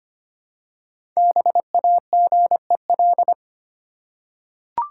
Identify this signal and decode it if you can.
Morse code, heard as a keyed tone, about 25 wpm, 700 Hz. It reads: BAGEL
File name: practice / 25wpm / Easy Words with Hints - 25wpm